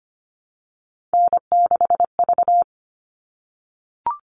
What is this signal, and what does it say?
Morse code, heard as a keyed tone, about 25 wpm, 700 Hz. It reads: N6V
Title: All Cut Numbers - 3-Digit 25wpm